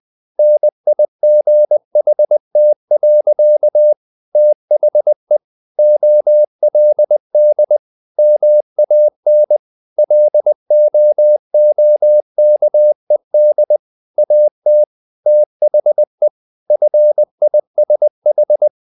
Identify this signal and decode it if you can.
Morse code, heard as a keyed tone, about 20 wpm, 600 Hz. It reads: NIGHT. THE OLD MAN LOOKED AT THE FISH
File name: the old man and the sea part 2 20wpm